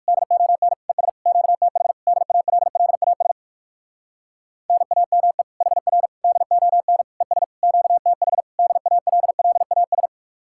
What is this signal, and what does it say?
Morse code, heard as a keyed tone, about 58 wpm, 700 Hz. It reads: DON ES QTH DALLAS NAME HR DON ES QTH DALLAS